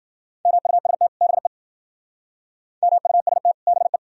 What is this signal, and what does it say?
Morse code, heard as a keyed tone, about 50 wpm, 700 Hz. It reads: MUST BE MUST BE
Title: Speed-Racing - Sets of 2 Words 40wpm